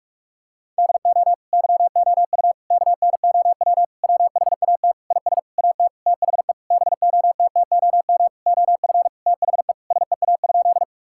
Morse code, heard as a keyed tone, about 46 wpm, 700 Hz. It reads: DO YOU KNOW WHAT IS AT THE BOTTOM OF THE SEA?
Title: Sentences from Top 1000 Words - Review Part 6 - 45wpm